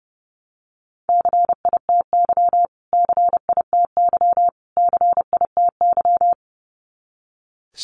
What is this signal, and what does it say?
Morse code, heard as a keyed tone, about 30 wpm, 700 Hz. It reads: CITY CITY CITY
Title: Top 300 Words Repeated 30wpm